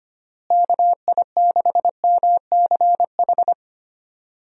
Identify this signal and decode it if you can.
Morse code, heard as a keyed tone, about 25 wpm, 700 Hz. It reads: KI6MCH
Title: USA Call Signs - 6 Characters - Repeated 2x - 25wpm